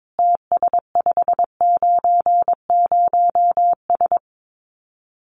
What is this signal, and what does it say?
Morse code, heard as a keyed tone, about 22 wpm, 700 Hz. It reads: TS590S
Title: Common QSO Phrases Encore 22wpm